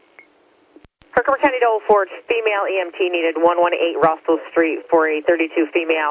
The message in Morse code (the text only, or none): none